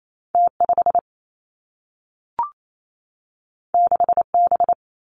Morse code, heard as a keyed tone, about 28 wpm, 700 Hz. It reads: T 5 6 B
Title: Two Challenging Letter-Number Combinations 28wpm